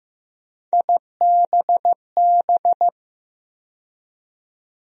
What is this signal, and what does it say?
Morse code, heard as a keyed tone, about 15 wpm, 700 Hz. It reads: IBB
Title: USA Call Sign Postfixes - 3 Letters 15wpm